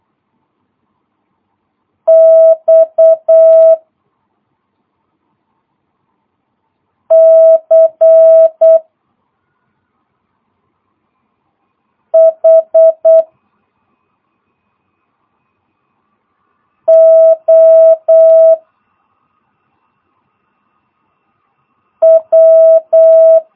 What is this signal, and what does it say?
Morse code, heard as a keyed tone, about 8 wpm, 650 Hz. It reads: X C H O W
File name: jh8cbh / 2022cw